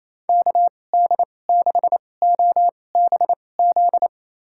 Morse code, heard as a keyed tone, about 28 wpm, spaced as for 20 wpm, 700 Hz. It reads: KD6OBZ
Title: USA Call Signs - 6 Characters - 2x Character Spacing - 28wpm